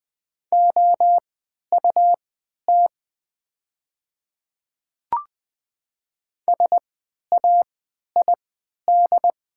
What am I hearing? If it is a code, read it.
Morse code, heard as a keyed tone, about 20 wpm, spaced as for 11 wpm, 700 Hz. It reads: OUT SAID
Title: 3x Character Spacing - Top 100 Words - 20wpm